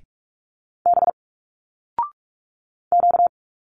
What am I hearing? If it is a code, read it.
Morse code, heard as a keyed tone, about 46 wpm, 700 Hz. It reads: B Q
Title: Single Letters 45wpm